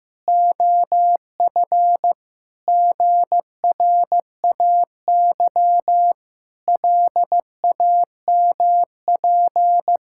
Morse code, heard as a keyed tone, about 15 wpm, 700 Hz. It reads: OF GRAY LAMP
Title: Pangram Challenge v2 - 15wpm